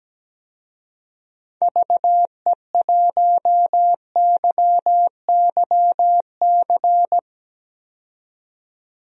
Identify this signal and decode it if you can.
Morse code, heard as a keyed tone, about 17 wpm, 700 Hz. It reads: VE1YYC
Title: DXCC-100 Call Signs - 6 Characters - 17wpm